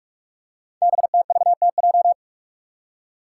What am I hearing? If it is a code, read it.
Morse code, heard as a keyed tone, about 44 wpm, 690 Hz. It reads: BTVTJ
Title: Code Groups - 5 Letters - 45wpm